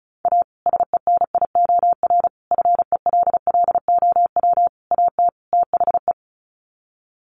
Morse code, heard as a keyed tone, about 34 wpm, 700 Hz. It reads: A SENIOR FELLOW AT THE
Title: Sets of 5 Words Encore 35wpm